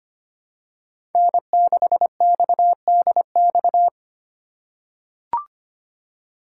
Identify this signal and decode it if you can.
Morse code, heard as a keyed tone, about 25 wpm, 700 Hz. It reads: N6XDX
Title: USA Call Signs - 5 Characters - 4x Character Spacing - 25wpm